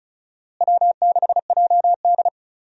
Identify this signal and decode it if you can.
Morse code, heard as a keyed tone, about 34 wpm, 700 Hz. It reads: W6JD
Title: USA Call Signs - 4 Characters - Repeated 3x - 35wpm